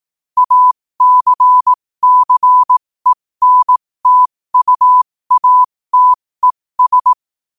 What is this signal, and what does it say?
Morse code, heard as a keyed tone, about 18 wpm, spaced as for 15 wpm, 1000 Hz. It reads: ACCENTUATES
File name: qotdfiles / 15wpm